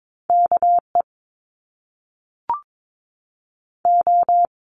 Morse code, heard as a keyed tone, about 22 wpm, 700 Hz. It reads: KE O
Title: Top Two Letter Combinations in Words 22wpm